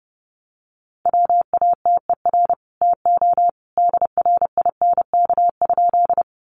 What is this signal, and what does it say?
Morse code, heard as a keyed tone, about 30 wpm, 700 Hz. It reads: WATER TO DRINK?
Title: Sentences from Top 1500 Words 30wpm